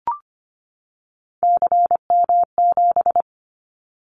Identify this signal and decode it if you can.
Morse code, heard as a keyed tone, about 25 wpm, 700 Hz. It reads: CM7
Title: Callsign Prefixes for Top 100 DXCC - 25wpm